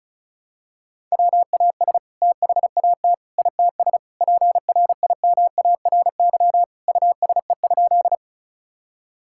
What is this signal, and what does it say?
Morse code, heard as a keyed tone, about 34 wpm, 700 Hz. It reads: WAS THAT ITS PRIMARY USE?
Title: Sentences from Top 1000 Words - Review Part 5 - 35wpm